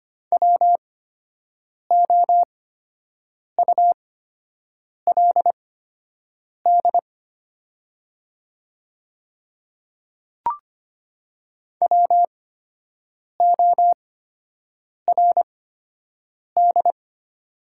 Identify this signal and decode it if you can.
Morse code, heard as a keyed tone, about 25 wpm, spaced as for 7 wpm, 700 Hz. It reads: WOULD WORD
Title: 8x Character Spacing - Top 100 Words - 25wpm